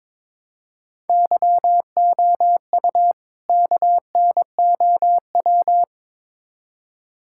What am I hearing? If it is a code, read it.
Morse code, heard as a keyed tone, about 22 wpm, 700 Hz. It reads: YOU KNOW
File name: Build A Phrase - 2-Word Phrases - 22wpm